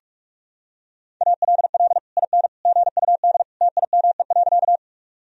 Morse code, heard as a keyed tone, about 46 wpm, 700 Hz. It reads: ALL IN GUD TIME.
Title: Sentences from Top 200 Words and Abbreviations 45wpm